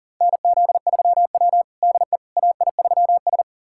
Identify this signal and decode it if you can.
Morse code, heard as a keyed tone, about 40 wpm, 700 Hz. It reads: NZ3W DE AI3S
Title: USA Callsign DE Callsign - 4 Characters - 40wpm